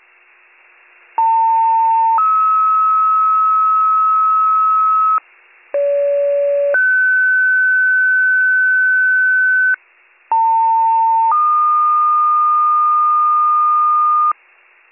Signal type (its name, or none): two-tone page